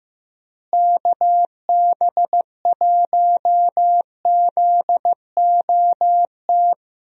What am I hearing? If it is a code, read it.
Morse code, heard as a keyed tone, about 15 wpm, 700 Hz. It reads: KB1ZOT
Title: USA Call Signs - 6 Characters - Repeated 3x - 15wpm